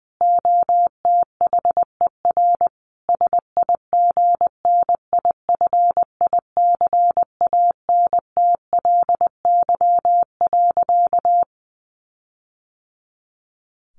Morse code, heard as a keyed tone, about 20 wpm, 700 Hz. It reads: OTHER SIGNIFICANTLY.
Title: Sentences from Top 1500 Words 20wpm